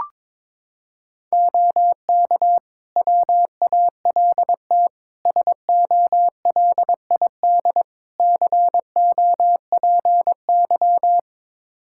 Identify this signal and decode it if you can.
Morse code, heard as a keyed tone, about 22 wpm, 700 Hz. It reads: OK WALT SOLID COPY
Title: Practice QSO - Exchange 3 - More Info - 22wpm